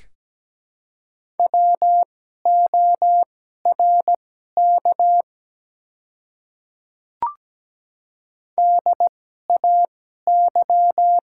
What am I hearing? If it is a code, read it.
Morse code, heard as a keyed tone, about 17 wpm, spaced as for 12 wpm, 700 Hz. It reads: WORK DAY
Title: 2x Character Spacing - Top 200 Words - 17wpm